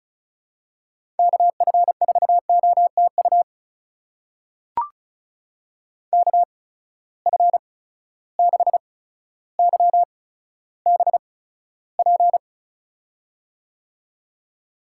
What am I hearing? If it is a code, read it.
Morse code, heard as a keyed tone, about 36 wpm, spaced as for 15 wpm, 700 Hz. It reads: KF4OTU KF6YBP
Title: USA Call Signs - 6 Characters - 8x Character Spacing - 35wpm